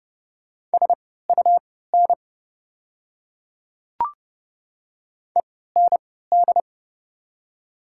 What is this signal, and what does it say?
Morse code, heard as a keyed tone, about 30 wpm, spaced as for 17 wpm, 700 Hz. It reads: SUN END
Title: 3x Character Spacing - Top 300 Words - 30wpm